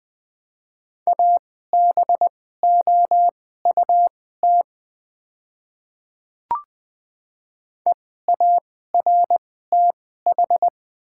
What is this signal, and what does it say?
Morse code, heard as a keyed tone, about 20 wpm, spaced as for 14 wpm, 700 Hz. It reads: ABOUT EARTH
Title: 2x Character Spacing - Top 200 Words - 20wpm